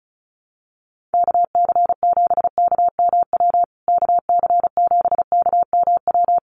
Morse code, heard as a keyed tone, about 34 wpm, 700 Hz. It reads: KC7KMW KC7KMW
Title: USA Call Signs Repeated 3x 35wpm